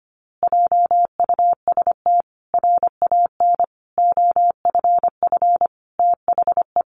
Morse code, heard as a keyed tone, about 25 wpm, 700 Hz. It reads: JUST RAN OFF THE
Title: A Sound Of Thunder 25wpm